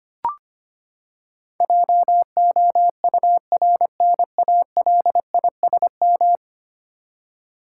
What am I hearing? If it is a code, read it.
Morse code, heard as a keyed tone, about 25 wpm, 700 Hz. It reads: JOURNALISM
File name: Ten Letter Words - 25wpm